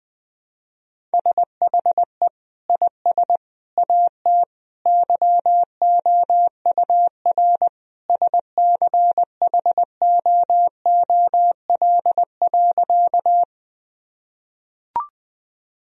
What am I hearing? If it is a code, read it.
Morse code, heard as a keyed tone, about 20 wpm, 700 Hz. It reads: SHE IS AT YOUR SCHOOL.
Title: Sentences from Top 1000 Words - Review Part 1 - 20wpm